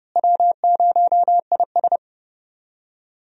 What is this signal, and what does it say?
Morse code, heard as a keyed tone, about 31 wpm, 700 Hz. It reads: W0IS
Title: USA Call Signs - 4 Characters - Repeated 3x - 30wpm